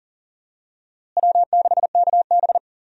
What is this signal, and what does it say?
Morse code, heard as a keyed tone, about 40 wpm, 700 Hz. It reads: W6KB
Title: USA Call Signs - 4 Characters - Repeated 3x - 40wpm